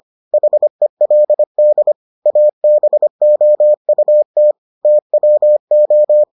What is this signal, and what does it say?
Morse code, heard as a keyed tone, about 25 wpm, 600 Hz. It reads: HELD ABOUT TWO